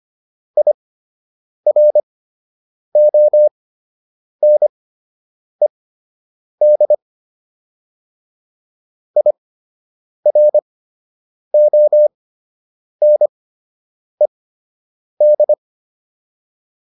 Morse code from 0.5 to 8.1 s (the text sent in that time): IRONED